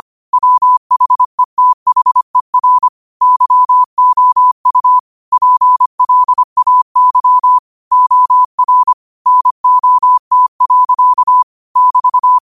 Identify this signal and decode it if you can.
Morse code, heard as a keyed tone, about 25 wpm, 1000 Hz. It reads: WHETHER YOU PLAY OR NOT. <BT>